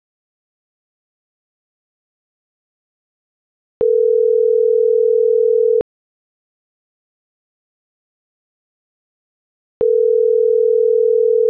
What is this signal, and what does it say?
Ringback tone (North America), 440 + 480 Hz.